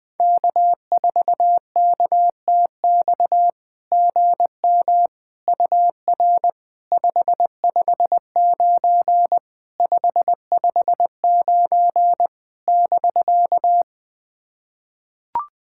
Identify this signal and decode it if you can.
Morse code, heard as a keyed tone, about 20 wpm, 690 Hz. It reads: K4KTX GM UR 559 559 <BK>
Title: SOTA QSO Practice 20wpm